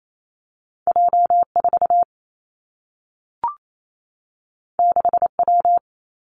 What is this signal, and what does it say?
Morse code, heard as a keyed tone, about 28 wpm, 700 Hz. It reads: J 4 6 W
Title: Two Challenging Letter-Number Combinations 28wpm